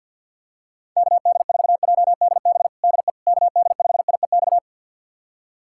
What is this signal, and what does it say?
Morse code, heard as a keyed tone, about 52 wpm, 700 Hz. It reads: KD4JDB DE KD5IEX
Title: USA Callsign DE Callsign 50wpm